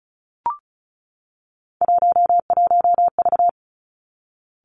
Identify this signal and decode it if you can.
Morse code, heard as a keyed tone, about 36 wpm, 700 Hz. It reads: 11V
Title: Three Challenging Letter-Number Combinations 35wpm